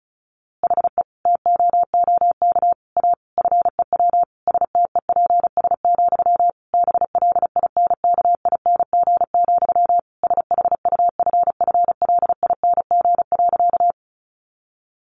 Morse code, heard as a keyed tone, about 34 wpm, 700 Hz. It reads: HE TOOK A FEW STEPS, BLINKING, SHUFFLING.
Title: A Sound Of Thunder 35wpm